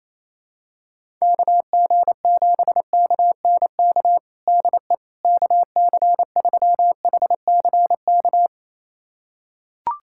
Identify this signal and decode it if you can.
Morse code, heard as a keyed tone, about 28 wpm, 700 Hz. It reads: KG7KNK DE KC3HCK
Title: USA Callsign DE Callsign - 6 Characters - 28wpm